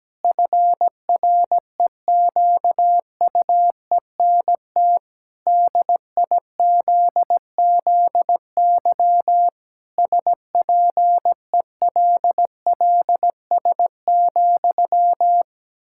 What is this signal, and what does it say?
Morse code, heard as a keyed tone, about 17 wpm, 700 Hz. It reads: FREQUENT DIZZY SPELLS,
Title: Pangram Challenge v2 - 17wpm